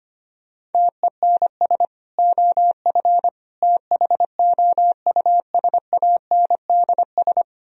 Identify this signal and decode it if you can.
Morse code, heard as a keyed tone, about 32 wpm, 700 Hz. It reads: TENS OF THOUSANDS